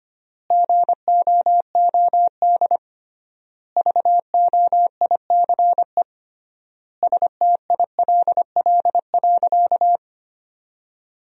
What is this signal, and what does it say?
Morse code, heard as a keyed tone, about 25 wpm, 700 Hz. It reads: GOOD VOICE STILL.
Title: 3x Word-Spacing - Sentences from Top 500 Words - 25wpm